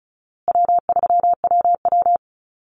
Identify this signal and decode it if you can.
Morse code, heard as a keyed tone, about 34 wpm, 700 Hz. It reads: W3WW
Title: Call Signs Repeated 2x - 35wpm